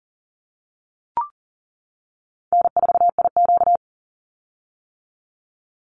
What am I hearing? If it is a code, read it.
Morse code, heard as a keyed tone, about 40 wpm, 700 Hz. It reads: N4IQ
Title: Call Signs - 40wpm